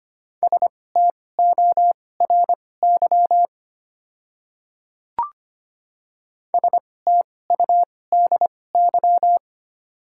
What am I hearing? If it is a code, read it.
Morse code, heard as a keyed tone, about 25 wpm, spaced as for 18 wpm, 700 Hz. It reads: STORY STUDY
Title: 2x Character Spacing - Top 300 Words - 25wpm